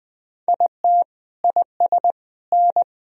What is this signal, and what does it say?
Morse code, heard as a keyed tone, about 21 wpm, 700 Hz. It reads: IT IS N
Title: Sentences from Top 500 Words and Abbreviations 20wpm